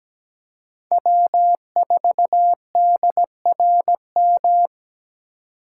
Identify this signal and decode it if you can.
Morse code, heard as a keyed tone, about 20 wpm, 700 Hz. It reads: W4DRM
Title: Speed Racing - USA 1x3 Call Signs 17wpm